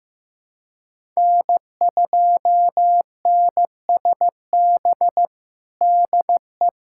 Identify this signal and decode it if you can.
Morse code, heard as a keyed tone, about 15 wpm, 700 Hz. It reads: N2NSB DE
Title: USA Callsign DE Callsign - 5 Characters - 15wpm